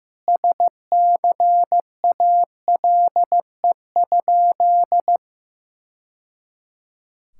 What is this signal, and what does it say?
Morse code, heard as a keyed tone, about 15 wpm, 700 Hz. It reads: SCALE?
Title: Sentences from Top 1000 Words - Review Part 5 - 15wpm